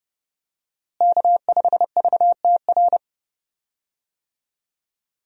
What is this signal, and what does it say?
Morse code, heard as a keyed tone, about 30 wpm, 700 Hz. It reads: K5VTR
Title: USA Call Signs - 5 Characters - Repeated 2x - 30wpm